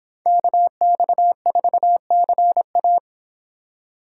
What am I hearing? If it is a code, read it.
Morse code, heard as a keyed tone, about 25 wpm, 700 Hz. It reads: KX4CA